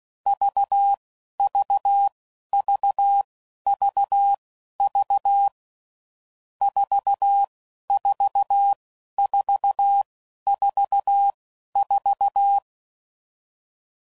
Morse code, heard as a keyed tone, about 16 wpm, spaced as for 11 wpm, 800 Hz. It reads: VVVVV 44444
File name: Combinations / Difficult Combos / AUV AUV4